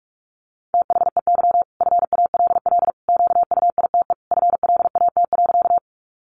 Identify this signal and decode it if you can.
Morse code, heard as a keyed tone, about 44 wpm, 700 Hz. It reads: THEY FALL QUITE FLAT.
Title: Sentences from Top 1000 Words 45wpm